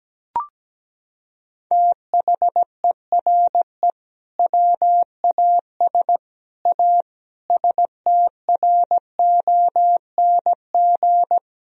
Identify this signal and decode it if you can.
Morse code, heard as a keyed tone, about 17 wpm, 700 Hz. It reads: THERE WAS A STRONG
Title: Sentences from Top 1000 Words - Review Part 3 - 17wpm